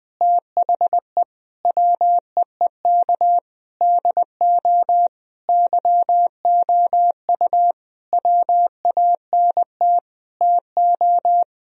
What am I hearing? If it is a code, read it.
Morse code, heard as a keyed tone, about 21 wpm, 700 Hz. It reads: THE WEEK DO YOU WANT TO
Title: Sentences from Top 1000 Words - Review Part 3 - 20wpm